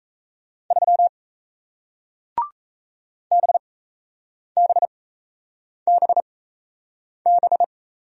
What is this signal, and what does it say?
Morse code, heard as a keyed tone, about 39 wpm, spaced as for 8 wpm, 700 Hz. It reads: 3 BBBB